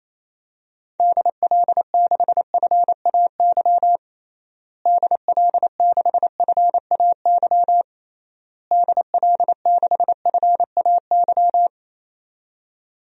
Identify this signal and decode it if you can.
Morse code, heard as a keyed tone, about 28 wpm, 700 Hz. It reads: DL6FAY DL6FAY DL6FAY
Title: DXCC-100 Call Signs - 6 Characters - Repeated 3x - 28wpm